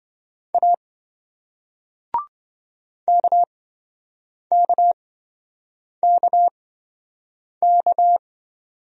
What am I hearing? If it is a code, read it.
Morse code, heard as a keyed tone, about 27 wpm, spaced as for 7 wpm, 700 Hz. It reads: A KKKK